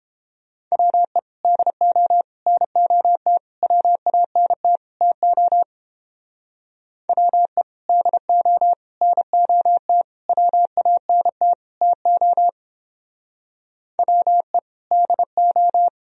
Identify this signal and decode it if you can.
Morse code, heard as a keyed tone, about 29 wpm, 700 Hz. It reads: WE DO NOT WANT TO WE DO NOT WANT TO WE DO